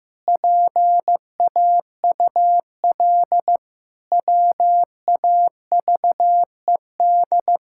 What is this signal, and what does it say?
Morse code, heard as a keyed tone, about 15 wpm, 700 Hz. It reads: PAUL WAVED
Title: Pangram Challenge v2 - 15wpm